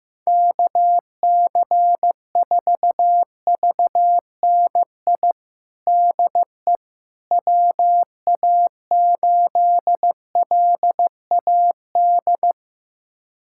Morse code, heard as a keyed tone, about 15 wpm, 700 Hz. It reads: KC4VNI DE WA8LAD
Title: USA Callsign DE Callsign 15wpm